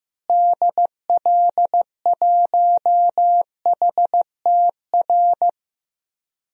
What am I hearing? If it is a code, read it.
Morse code, heard as a keyed tone, about 16 wpm, 700 Hz. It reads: DL1HTR
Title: DXCC-100 Call Signs - 6 Characters - Repeated 2x - 15wpm